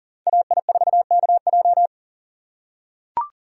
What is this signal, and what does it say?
Morse code, heard as a keyed tone, about 40 wpm, 700 Hz. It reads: AI4KJ
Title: USA Call Signs - 5 Characters - 2x Character Spacing - 40wpm